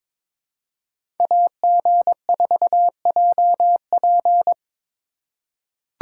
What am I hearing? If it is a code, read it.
Morse code, heard as a keyed tone, about 21 wpm, 700 Hz. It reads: AG4JP
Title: USA Call Signs - 5 Characters - 4x Character Spacing - 22wpm